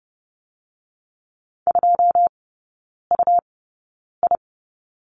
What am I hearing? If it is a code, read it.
Morse code, heard as a keyed tone, about 30 wpm, spaced as for 9 wpm, 700 Hz. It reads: 2UI
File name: Lesson 17e - Letter u - Three Characters 30wpm